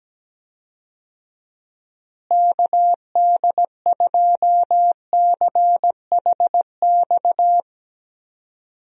Morse code, heard as a keyed tone, about 17 wpm, 700 Hz. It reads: KD2CHX
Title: USA Call Signs - 6 Characters - Repeated 2x - 17wpm